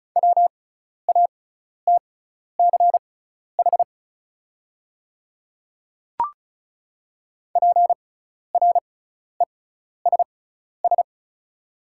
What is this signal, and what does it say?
Morse code, heard as a keyed tone, about 36 wpm, spaced as for 12 wpm, 690 Hz. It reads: WATCH PRESS
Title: 6x Character Spacing - Top 500 Words - 35wpm